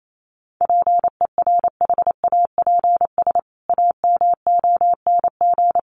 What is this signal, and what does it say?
Morse code, heard as a keyed tone, about 28 wpm, 700 Hz. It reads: PERHAPS AMONG
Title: Sentences from Top 1500 Words 28wpm